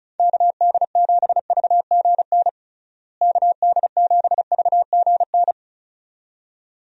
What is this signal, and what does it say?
Morse code, heard as a keyed tone, about 36 wpm, 700 Hz. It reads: KD7VGN KD7VGN
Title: USA Call Signs - 6 Characters - Repeated 2x - 35wpm